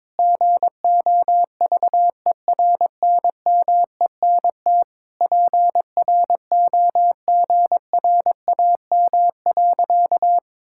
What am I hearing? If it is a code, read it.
Morse code, heard as a keyed tone, about 21 wpm, 700 Hz. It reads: GOVERNMENT PROGRAM.
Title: Sentences from Top 1000 Words - Review Part 5 - 22wpm